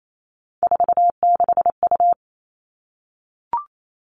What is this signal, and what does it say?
Morse code, heard as a keyed tone, about 28 wpm, 700 Hz. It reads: 46U
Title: All Cut Numbers - 3-Digit 28wpm